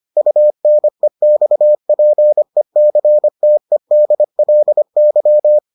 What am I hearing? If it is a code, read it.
Morse code, heard as a keyed tone, about 25 wpm, 600 Hz. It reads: UNEXPECTEDLY